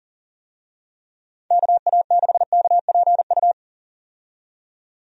Morse code, heard as a keyed tone, about 40 wpm, 700 Hz. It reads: KA6KPU
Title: USA Call Signs - 6 Characters - 40wpm